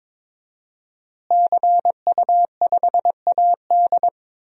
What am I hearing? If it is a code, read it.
Morse code, heard as a keyed tone, about 21 wpm, 700 Hz. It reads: CU5AD
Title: DXCC-100 Call Signs - 5 Characters - Repeated 2x - 22wpm